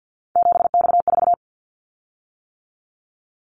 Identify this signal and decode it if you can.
Morse code, heard as a keyed tone, about 49 wpm, 700 Hz. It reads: 7X4
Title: Callsign Prefixes for Top 100 DXCC - 50wpm